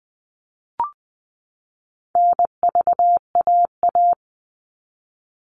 Morse code, heard as a keyed tone, about 20 wpm, 700 Hz. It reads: NVAA
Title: All Cut Numbers - 4-Digit 20wpm